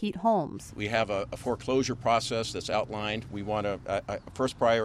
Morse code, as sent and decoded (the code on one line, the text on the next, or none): none